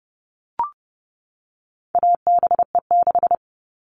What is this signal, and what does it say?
Morse code, heard as a keyed tone, about 30 wpm, 700 Hz. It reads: ABE6
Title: All Cut Numbers - 4-Digit 30wpm